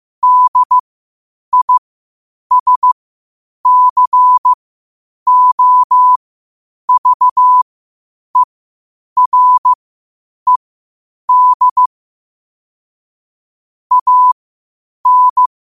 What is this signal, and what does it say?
Morse code, heard as a keyed tone, about 15 wpm, spaced as for 8 wpm, 1000 Hz. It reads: DISCOVERED AN
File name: qotdfiles / 8wpm